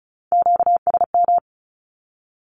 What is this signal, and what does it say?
Morse code, heard as a keyed tone, about 36 wpm, 700 Hz. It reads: QSM
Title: Q-Codes with Spoken Meaning - 35wpm